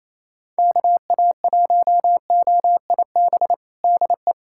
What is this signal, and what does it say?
Morse code, heard as a keyed tone, about 28 wpm, 700 Hz. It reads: KA1OIB DE
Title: USA Callsign DE Callsign 28wpm